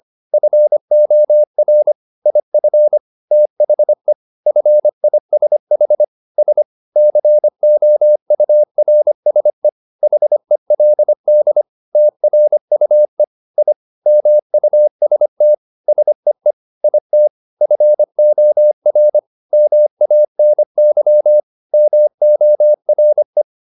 Morse code, heard as a keyed tone, about 25 wpm, 600 Hz. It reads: FOR IF THE FISH S COURSE HELD TRUE I MUST SEE IT FOR MANY MORE
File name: the old man and the sea part 1 25wpm